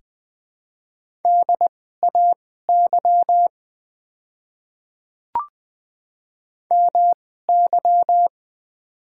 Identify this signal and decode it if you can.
Morse code, heard as a keyed tone, about 20 wpm, spaced as for 14 wpm, 700 Hz. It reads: DAY MY
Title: 2x Character Spacing - Top 100 Words - 20wpm